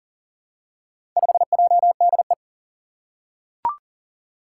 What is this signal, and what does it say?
Morse code, heard as a keyed tone, about 40 wpm, 700 Hz. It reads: 5JDE
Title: Four Challenging Letter-Number Combinations 40wpm